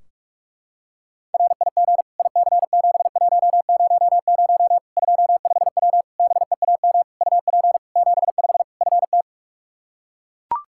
Morse code, heard as a keyed tone, about 42 wpm, 700 Hz. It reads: RIG IC7100 25W BEAM UP 75 FT
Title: Practice QSO - Exchange 2 - Info - 45wpm